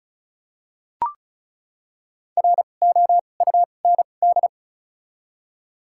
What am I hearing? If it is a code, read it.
Morse code, heard as a keyed tone, about 36 wpm, spaced as for 25 wpm, 700 Hz. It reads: ROUND
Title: 2x Character Spacing - Top 300 Words - 35wpm